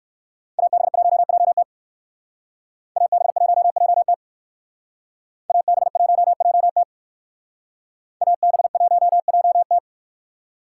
Morse code, heard as a keyed tone, about 57 wpm, 700 Hz. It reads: AB1JT AB1JT AB1JT AB1JT